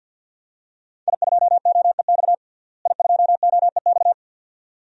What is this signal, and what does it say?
Morse code, heard as a keyed tone, about 51 wpm, 700 Hz. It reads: I2OEX I2OEX
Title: DXCC-100 Call Signs - 5 Characters - Repeated 2x - 50wpm